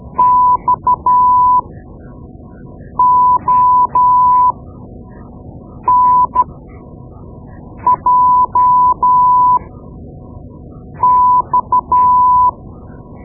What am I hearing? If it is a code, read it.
Morse code, heard as a keyed tone, about 12 wpm, spaced as for 5 wpm, 1000 Hz. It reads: XONJX
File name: Morse Lessons / Morse Lessons / Tape-2B